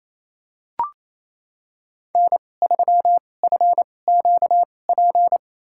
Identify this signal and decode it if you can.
Morse code, heard as a keyed tone, about 28 wpm, spaced as for 20 wpm, 700 Hz. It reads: N3FQP
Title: USA Call Signs - 5 Characters - 2x Character Spacing - 28wpm